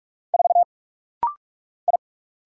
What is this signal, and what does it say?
Morse code, heard as a keyed tone, about 46 wpm, 700 Hz. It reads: 4 I
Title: Single Letter-Number Alt - Rapid-Fire 45wpm